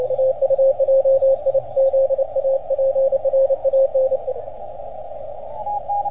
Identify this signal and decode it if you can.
Morse code, heard as a keyed tone, about 28 wpm, 540 Hz, with radio noise. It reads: TU JIZA PRAN